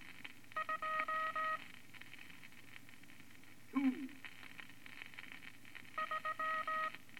Morse code, heard as a keyed tone, about 19 wpm, 1260 Hz, with noise. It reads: 2 3
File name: Morse Code Records / 14A LCM 156 Units V Single Cont